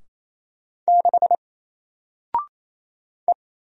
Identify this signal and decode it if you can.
Morse code, heard as a keyed tone, about 27 wpm, 700 Hz. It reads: 6 E
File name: Single-Letter-Number-28wpm